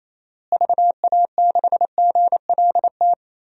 Morse code, heard as a keyed tone, about 28 wpm, 700 Hz. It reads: VA6GLT
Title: DXCC-100 Call Signs - 6 Characters - Repeated 2x - 28wpm